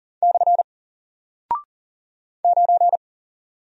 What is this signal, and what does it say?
Morse code, heard as a keyed tone, about 40 wpm, 700 Hz. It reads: / 9
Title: Single Character (Letter:Number:Punctuation:Prosign) Alt 40wpm